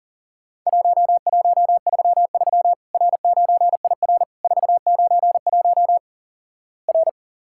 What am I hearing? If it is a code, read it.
Morse code, heard as a keyed tone, about 44 wpm, 700 Hz. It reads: 1133 R9IR 491 R
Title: Worked All Europe DX Contest - QTC Exchange - 40wpm